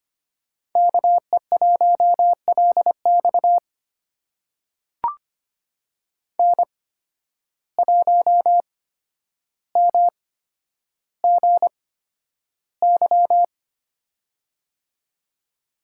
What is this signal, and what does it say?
Morse code, heard as a keyed tone, about 25 wpm, spaced as for 11 wpm, 700 Hz. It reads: KE1LX N1MGY
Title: USA Call Signs - 5 Characters - 8x Character Spacing - 25wpm